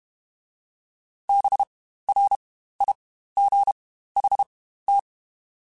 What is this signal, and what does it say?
Morse code, heard as a keyed tone, about 32 wpm, spaced as for 15 wpm, 800 Hz. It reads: BRIGHT
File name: Spaced Characters / Spaced Characters-mp3s / Weather 5 5 sp @30